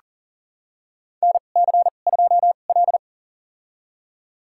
Morse code, heard as a keyed tone, about 40 wpm, spaced as for 29 wpm, 700 Hz. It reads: NC2L
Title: USA Call Signs - 2x Character Spacing - 40wpm